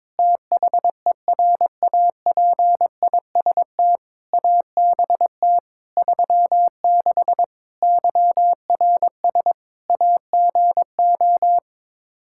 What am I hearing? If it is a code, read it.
Morse code, heard as a keyed tone, about 21 wpm, 700 Hz. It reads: THERAPIST ABT 36 YRS AGO
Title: Practice QSO - Exchange 3 - More Info - 22wpm